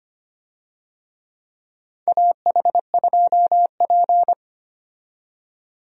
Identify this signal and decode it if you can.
Morse code, heard as a keyed tone, about 25 wpm, 700 Hz. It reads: AH2P
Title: USA Call Signs - 4 Characters - 25wpm